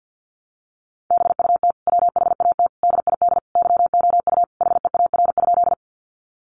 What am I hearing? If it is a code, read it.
Morse code, heard as a keyed tone, about 50 wpm, 700 Hz. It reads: BUT WHAT DID YOU HEAR?
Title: Sentences from Top 300 Words 50wpm